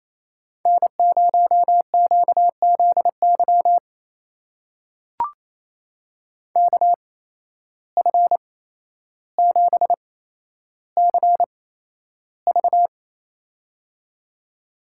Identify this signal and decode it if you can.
Morse code, heard as a keyed tone, about 29 wpm, spaced as for 12 wpm, 700 Hz. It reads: N0QZY KF7CV
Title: USA Call Signs - 5 Characters - 8x Character Spacing - 28wpm